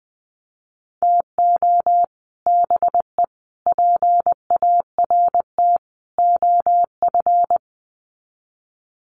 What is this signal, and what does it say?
Morse code, heard as a keyed tone, about 20 wpm, 700 Hz. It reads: TO BE PART OF
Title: Sets of 4 Words 20wpm